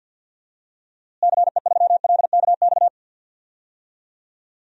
Morse code, heard as a keyed tone, about 53 wpm, 700 Hz. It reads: CE3LKX
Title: DXCC-100 Call Signs - 6 Characters - Repeated 2x - 50wpm